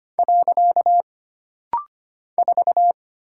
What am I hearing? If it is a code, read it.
Morse code, heard as a keyed tone, about 25 wpm, 700 Hz. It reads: . 4
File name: Single Character (Letter:Number:Punctuation:Prosign) - Rapid-Fire - 25wpm